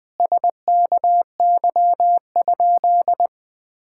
Morse code, heard as a keyed tone, about 21 wpm, 700 Hz. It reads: SKY?
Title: Sentences with 4-Word Phrases - Spoken Beginning - 20wpm